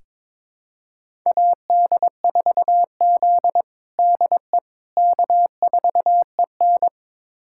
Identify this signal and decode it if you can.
Morse code, heard as a keyed tone, about 22 wpm, 700 Hz. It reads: AD4Z DE K4EN
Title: USA Callsign DE Callsign - 4 Characters - 22wpm